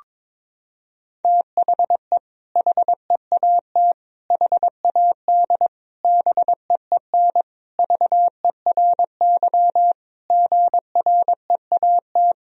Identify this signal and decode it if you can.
Morse code, heard as a keyed tone, about 22 wpm, 700 Hz. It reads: THE HEAT HAD BEEN VERY GREAT
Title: Sentences from Top 1000 Words - Review Part 3 - 22wpm